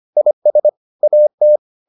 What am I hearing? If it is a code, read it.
Morse code, heard as a keyed tone, about 25 wpm, 600 Hz. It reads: IS AT